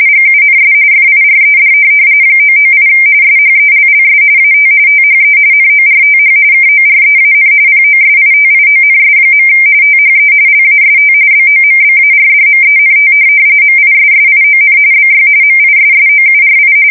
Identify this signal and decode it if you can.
RTTY, about 45.45 baud. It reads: GUNS FIRED 998 SHELLS AT THE NORTH KOREAN POSITIONS. THE LAST BOMBARDMENT MISSION BY MISSOURI
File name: fsk170 / USS-Missouri